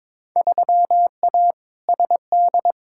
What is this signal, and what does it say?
Morse code, heard as a keyed tone, about 22 wpm, 700 Hz. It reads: 3A SD
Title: Field Day Exchanges - 22wpm